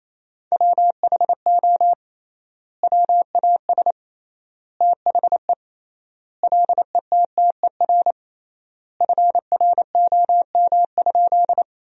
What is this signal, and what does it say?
Morse code, heard as a keyed tone, about 28 wpm, 700 Hz. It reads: WHO WAS THE LETTER FROM?
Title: 3x Word-Spacing - Sentences from Top 300 Words - 28wpm